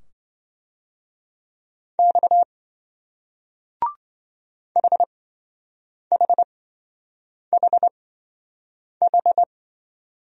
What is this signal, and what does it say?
Morse code, heard as a keyed tone, about 26 wpm, spaced as for 7 wpm, 700 Hz. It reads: X HHHH